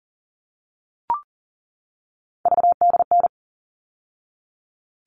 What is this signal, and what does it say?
Morse code, heard as a keyed tone, about 40 wpm, 700 Hz. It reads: VDN
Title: All Cut Numbers - 3-Digit 40wpm